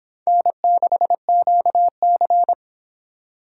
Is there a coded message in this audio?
Yes: Morse code from 0.3 to 2.5 s, heard as a keyed tone, about 26 wpm, 700 Hz. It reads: N6QC